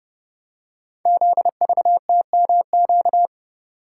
Morse code, heard as a keyed tone, about 30 wpm, 700 Hz. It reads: ZVTMQ
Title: Code Groups - 5 Letters - 30wpm